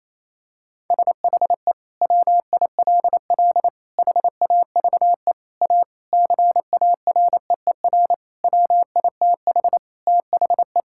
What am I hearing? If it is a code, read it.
Morse code, heard as a keyed tone, about 27 wpm, 700 Hz. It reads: SHE WILL HAVE A CAREER WITH THE
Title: Sentences from Top 1000 Words - Review Part 6 - 28wpm